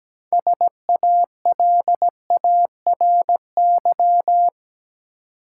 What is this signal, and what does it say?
Morse code, heard as a keyed tone, about 17 wpm, 700 Hz. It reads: SALARY
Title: Six Letter Words - 17wpm